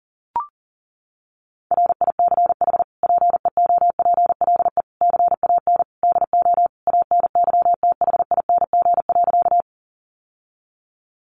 Morse code, heard as a keyed tone, about 40 wpm, 700 Hz. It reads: RICH PEOPLE CAN DO ANYTHING.
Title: Sentences from Top 1100 Words 40wpm